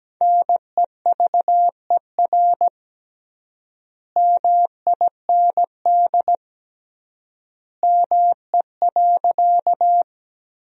Morse code, heard as a keyed tone, about 17 wpm, 700 Hz. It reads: NEVER MIND ME.
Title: 3x Word-Spacing - Sentences from Top 500 Words - 17wpm